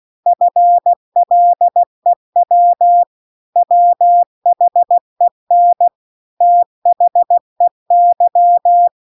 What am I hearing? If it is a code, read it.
Morse code, heard as a keyed tone, about 16 wpm, 700 Hz. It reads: FLEW WHEN THEY
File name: CW-The-Big-Twoheaded-River-2-16WPM